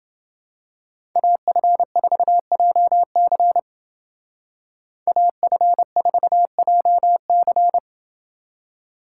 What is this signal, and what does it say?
Morse code, heard as a keyed tone, about 28 wpm, 700 Hz. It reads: AF4JC AF4JC